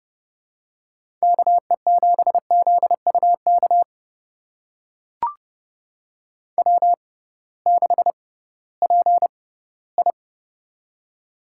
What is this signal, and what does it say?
Morse code, heard as a keyed tone, about 30 wpm, spaced as for 18 wpm, 700 Hz. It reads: KE7ZUK W6PI
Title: USA Call Signs - 6x Character Spacing - 30wpm